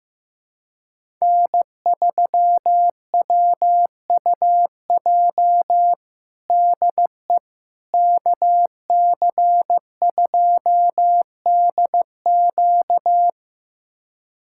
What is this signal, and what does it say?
Morse code, heard as a keyed tone, about 15 wpm, 700 Hz. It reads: N3WUJ DE KC2DQ
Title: USA Callsign DE Callsign - 5 Characters - 15wpm